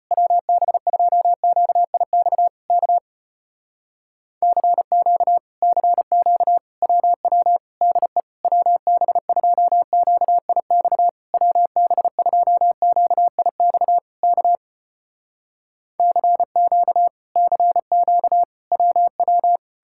Morse code, heard as a keyed tone, about 35 wpm, 700 Hz. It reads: WB2QIX K CQ CQ WW DE WB2QIX WB2QIX K CQ CQ WW